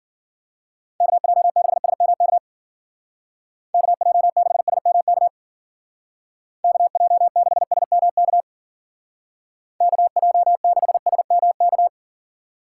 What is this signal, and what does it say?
Morse code, heard as a keyed tone, about 50 wpm, 700 Hz. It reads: KJ6SMK KJ6SMK KJ6SMK KJ6SMK